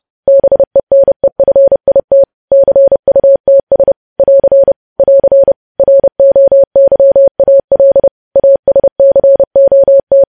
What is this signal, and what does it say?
Morse code, heard as a keyed tone, about 30 wpm, 550 Hz. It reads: BENEFIT CUTS <AR> <AR> ROYAL ASCOT
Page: news headlines in morse code at 30 wpm